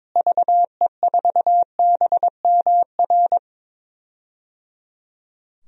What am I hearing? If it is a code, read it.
Morse code, heard as a keyed tone, about 22 wpm, 700 Hz. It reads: VE4BMR
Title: DXCC-100 Call Signs - 6 Characters - Repeated 3x - 22wpm